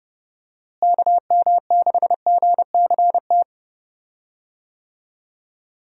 Morse code, heard as a keyed tone, about 30 wpm, 700 Hz. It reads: KM6GCT
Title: USA Call Signs - 6 Characters - Repeated 2x - 30wpm